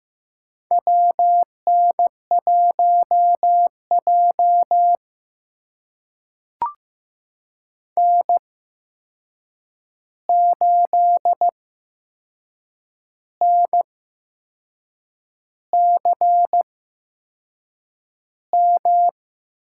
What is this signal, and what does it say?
Morse code, heard as a keyed tone, about 15 wpm, spaced as for 6 wpm, 700 Hz. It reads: WN1J N8NCM
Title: USA Call Signs - 8x Character Spacing - 15wpm